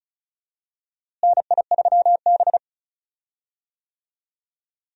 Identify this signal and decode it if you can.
Morse code, heard as a keyed tone, about 36 wpm, 700 Hz. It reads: NI3B
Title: USA Call Signs - 4 Characters - Repeated 2x - 35wpm